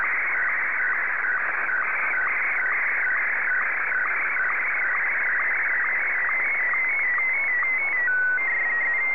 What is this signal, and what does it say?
SSTV (slow-scan television), Martin 1.
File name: sstv / M1-VK3